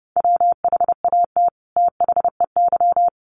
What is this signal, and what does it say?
Morse code, heard as a keyed tone, about 30 wpm, 700 Hz. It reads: WHAT THEY
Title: Sets of 2 Words 30wpm